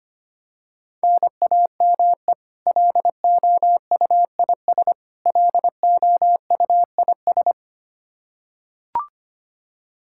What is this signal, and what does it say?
Morse code, heard as a keyed tone, about 25 wpm, 700 Hz. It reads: NAME LOUIS LOUIS
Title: K1USN SST QSO Practice 25wpm